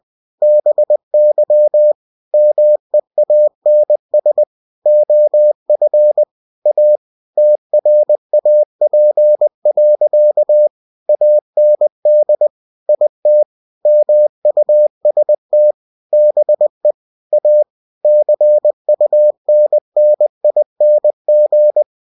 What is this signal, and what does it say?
Morse code, heard as a keyed tone, about 20 wpm, 600 Hz. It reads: BY MEANS OF A TRAP. AND IT MUST BE A CUNNING
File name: cw-books / winnie-the-pooh / winnie-20wpm-page-0034